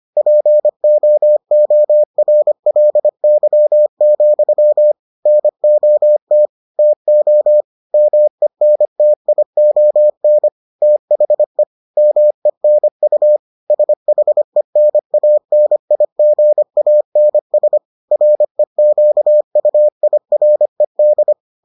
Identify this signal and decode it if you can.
Morse code, heard as a keyed tone, about 25 wpm, 600 Hz. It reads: POORLY, NOT TO MENTION THE MENU SHENANIGANS REQUIRED